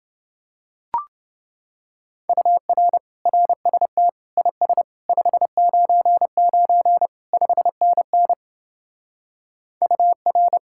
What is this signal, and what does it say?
Morse code, heard as a keyed tone, about 29 wpm, 700 Hz. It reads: UR RST IS 599 5NN UR